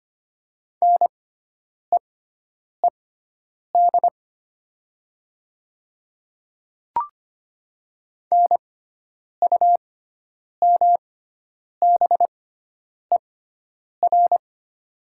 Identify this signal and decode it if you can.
Morse code, heard as a keyed tone, about 25 wpm, spaced as for 9 wpm, 700 Hz. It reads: NEED NUMBER
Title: 6x Character Spacing - Top 300 Words - 25wpm